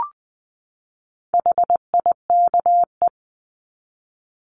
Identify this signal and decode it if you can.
Morse code, heard as a keyed tone, about 20 wpm, 700 Hz. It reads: HIKE
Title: 260 Lesson 42e - Words 20-20wpm